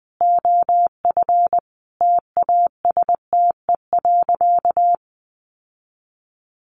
Morse code, heard as a keyed tone, about 20 wpm, 700 Hz. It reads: OF TASTE.
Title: Sentences from Top 1100 Words 20wpm